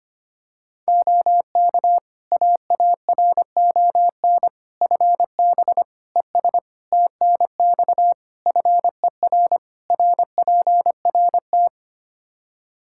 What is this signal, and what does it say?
Morse code, heard as a keyed tone, about 25 wpm, 700 Hz. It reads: OK AARON FB ES TNX FER RPRT
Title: Practice QSO - Exchange 2 - Info - 25wpm